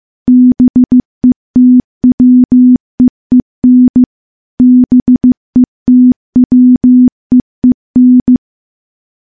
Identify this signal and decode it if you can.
Morse code, heard as a keyed tone, about 15 wpm, 250 Hz. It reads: BETWEEN BETWEEN
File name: morse-training-15WPM